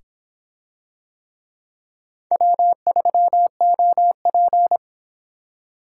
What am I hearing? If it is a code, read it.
Morse code, heard as a keyed tone, about 26 wpm, 700 Hz. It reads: W3OP